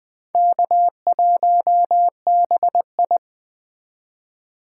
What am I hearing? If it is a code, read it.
Morse code, heard as a keyed tone, about 20 wpm, 700 Hz. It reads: K1BI
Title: Speed Racing - USA Call Signs 17wpm